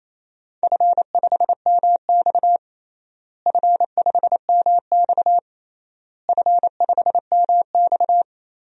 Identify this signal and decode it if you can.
Morse code, heard as a keyed tone, about 28 wpm, 700 Hz. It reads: F5MX F5MX F5MX
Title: DXCC-100 Call Signs - 4 Characters - Repeated 3x - 28wpm